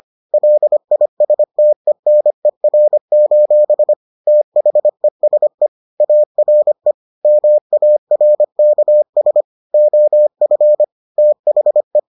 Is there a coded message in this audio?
Yes: Morse code from 0.3 to 12.0 s, heard as a keyed tone, about 25 wpm, 600 Hz. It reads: LISTENER: THESE ARE MARKS OF THE